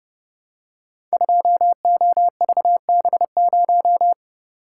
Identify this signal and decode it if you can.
Morse code, heard as a keyed tone, about 29 wpm, 700 Hz. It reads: 2OVB0
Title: Code Groups - 5 Letter:Numbers - 30wpm